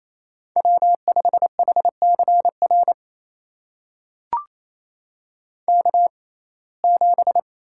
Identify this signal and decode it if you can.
Morse code, heard as a keyed tone, about 28 wpm, spaced as for 20 wpm, 700 Hz. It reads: W5HCR K7
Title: USA Call Signs - 5 Characters - 6x Character Spacing - 28wpm